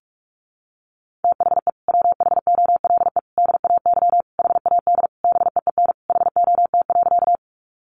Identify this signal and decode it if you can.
Morse code, heard as a keyed tone, about 44 wpm, 700 Hz. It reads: THE WHOLE DAY HAD BEEN HOT.
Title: Sentences from Top 500 Words 45wpm